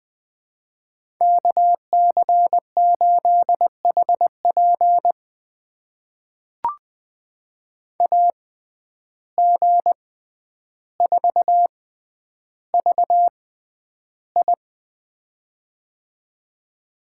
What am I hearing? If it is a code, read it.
Morse code, heard as a keyed tone, about 20 wpm, spaced as for 10 wpm, 700 Hz. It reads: KC8HP AG4VI
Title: USA Call Signs - 5 Characters - 6x Character Spacing - 20wpm